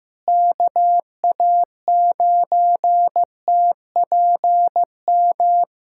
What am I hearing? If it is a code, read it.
Morse code, heard as a keyed tone, about 16 wpm, 700 Hz. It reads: KA9TPM
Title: USA Callsign DE Callsign - 6 Characters - 15wpm